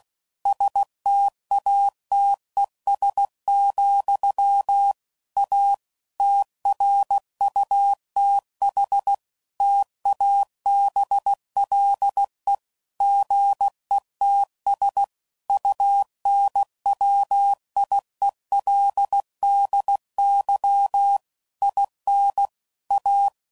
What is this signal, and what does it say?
Morse code, heard as a keyed tone, about 16 wpm, 800 Hz. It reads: STATES, A TRUTH TABLE GETS UNWIELDY IN A
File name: CW Code Groups @15WPM / Technical Subjects / AND Gate